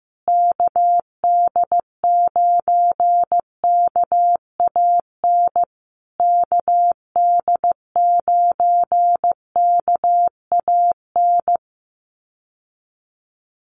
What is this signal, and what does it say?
Morse code, heard as a keyed tone, about 15 wpm, 700 Hz. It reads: KD9KAN KD9KAN
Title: USA Call Signs Repeated 3x 15wpm